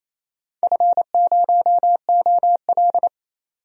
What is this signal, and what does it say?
Morse code, heard as a keyed tone, about 27 wpm, 700 Hz. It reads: F0OL
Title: DXCC-100 Call Signs - 4 Characters - 28wpm